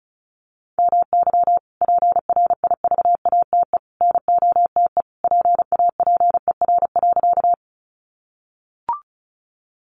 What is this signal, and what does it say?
Morse code, heard as a keyed tone, about 34 wpm, 700 Hz. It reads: MY PRIVATE NOTE PAPER.
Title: Sentences from Top 1000 Words 35wpm